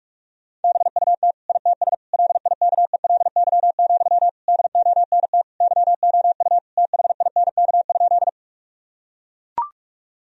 Morse code, heard as a keyed tone, about 41 wpm, 700 Hz. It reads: BUT ITS LIKELY, DONT YOU THINK?
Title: Sentences from Top 1000 Words - Review Part 4 - 45wpm